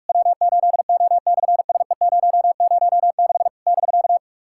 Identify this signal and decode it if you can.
Morse code, heard as a keyed tone, about 42 wpm, 700 Hz. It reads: W8O/SE006 <BK>
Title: SOTA QSO Practice 45wpm